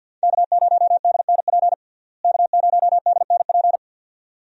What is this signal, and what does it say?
Morse code, heard as a keyed tone, about 52 wpm, 700 Hz. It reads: K0DNP K0DNP
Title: USA Call Signs - 5 Characters - Repeated 2x - 50wpm